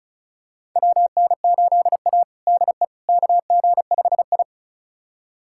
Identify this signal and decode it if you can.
Morse code, heard as a keyed tone, about 36 wpm, 700 Hz. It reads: WN8A DE KG5I
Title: USA Callsign DE Callsign - 4 Characters - 35wpm